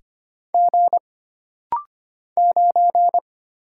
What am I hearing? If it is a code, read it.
Morse code, heard as a keyed tone, about 25 wpm, 700 Hz. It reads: G 9
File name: Single Character (Letter:Number:Punctuation:Prosign) - Rapid-Fire - 25wpm